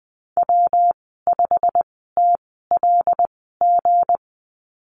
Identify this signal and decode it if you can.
Morse code, heard as a keyed tone, about 20 wpm, spaced as for 15 wpm, 700 Hz. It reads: W5TLG
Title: 167 Lesson 27g - Number 7 - Call Signs 20wpm